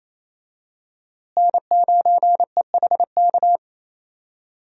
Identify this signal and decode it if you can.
Morse code, heard as a keyed tone, about 28 wpm, 700 Hz. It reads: N9EHK
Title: USA Call Signs - 5 Characters - Repeated 3x - 28wpm